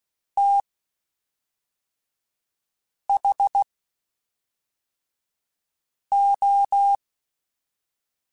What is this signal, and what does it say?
Morse code, heard as a keyed tone, about 16 wpm, spaced as for 3 wpm, 800 Hz. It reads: THO
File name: Quotes Spaced / Quotes Spaced-mp3s / IgnoringFacts sp @5